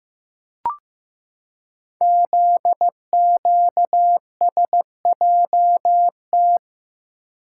Morse code, heard as a keyed tone, about 15 wpm, 700 Hz. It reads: ZQSJT
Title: Code Groups - 5 Letters - 15wpm